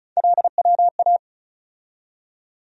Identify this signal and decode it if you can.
Morse code, heard as a keyed tone, about 34 wpm, 700 Hz. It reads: LWA
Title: USA Call Sign Postfixes - 3 Letters 35wpm